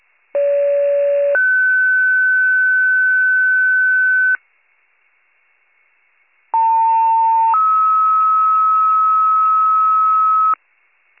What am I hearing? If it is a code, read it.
Two-tone page, heard as a one-second tone, then a three-second tone, about 570 then 1500 Hz, about 900 then 1290 Hz. It is heard from 0.3 to 10.5 s.